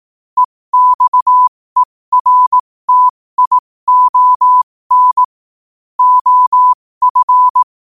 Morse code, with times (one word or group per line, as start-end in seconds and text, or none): 0.4-5.2 EXERTION
6.0-7.6 OF